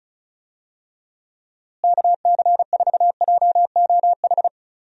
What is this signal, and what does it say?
Morse code, heard as a keyed tone, about 36 wpm, 700 Hz. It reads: KC4JOH
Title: USA Call Signs - 6 Characters - Repeated 2x - 35wpm